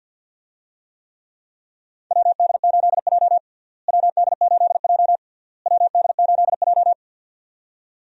Morse code, heard as a keyed tone, about 52 wpm, 700 Hz. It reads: WD8J WD8J WD8J
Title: USA Call Signs - 4 Characters - Repeated 3x - 50wpm